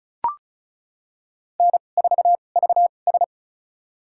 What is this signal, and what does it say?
Morse code, heard as a keyed tone, about 36 wpm, spaced as for 25 wpm, 700 Hz. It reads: N4VS
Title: USA Call Signs - 4 Characters - 2x Character Spacing - 35wpm